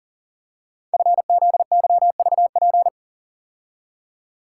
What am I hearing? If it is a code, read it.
Morse code, heard as a keyed tone, about 39 wpm, 690 Hz. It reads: FZYVP
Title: Code Groups - 5 Letter:Numbers - 40wpm